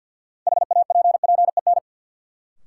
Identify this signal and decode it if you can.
Morse code, heard as a keyed tone, about 53 wpm, 700 Hz. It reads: HAPPEN
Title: Sets of 5 Words Encore - New Word Drill - 50wpm